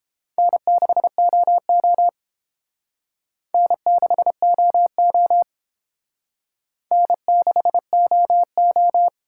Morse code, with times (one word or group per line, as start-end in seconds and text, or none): 0.4-2.1 N6OO
3.5-5.4 N6OO
6.9-9.1 N6OO